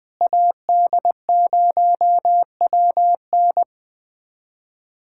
Morse code, heard as a keyed tone, about 20 wpm, 690 Hz. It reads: AD0WN
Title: POTA QSO Practice - 20wpm